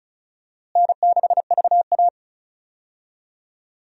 Morse code, heard as a keyed tone, about 36 wpm, 700 Hz. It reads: N6VA
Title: USA Call Signs - 4 Characters - Repeated 3x - 35wpm